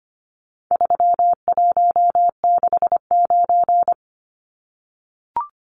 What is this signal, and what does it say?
Morse code, heard as a keyed tone, about 25 wpm, 700 Hz. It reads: 3169
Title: Numbers - 4-Digit 25wpm